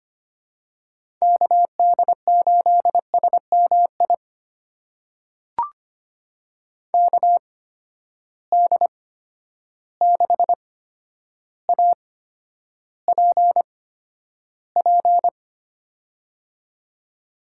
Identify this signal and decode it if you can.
Morse code, heard as a keyed tone, about 25 wpm, spaced as for 11 wpm, 700 Hz. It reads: KD8SMI KD6APP
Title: USA Call Signs - 6 Characters - 8x Character Spacing - 25wpm